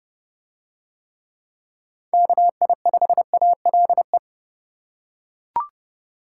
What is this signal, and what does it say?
Morse code, heard as a keyed tone, about 29 wpm, 700 Hz. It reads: KI5ALE